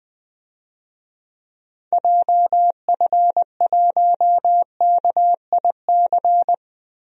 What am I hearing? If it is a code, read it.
Morse code, heard as a keyed tone, about 20 wpm, 700 Hz. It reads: JF1KIC
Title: DXCC-100 Call Signs - 6 Characters - Repeated 3x - 20wpm